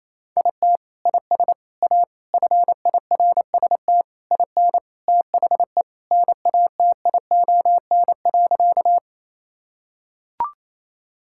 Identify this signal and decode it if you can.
Morse code, heard as a keyed tone, about 27 wpm, 700 Hz. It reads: IT IS A FIRST IN THE NATION.
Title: Sentences from Top 1000 Words - Review Part 6 - 28wpm